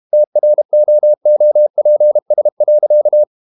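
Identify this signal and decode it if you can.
Morse code, heard as a keyed tone, about 32 wpm, 600 Hz. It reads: TROOPS.
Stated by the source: Morse code practice at 32 wpm from CW med Gnister, reading Ken Follett's novel Never in English.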